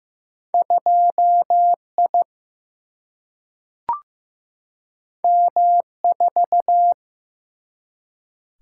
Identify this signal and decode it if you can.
Morse code, heard as a keyed tone, about 15 wpm, 700 Hz. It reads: 2 I M 4
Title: Single-Letter Single-Number 15wpm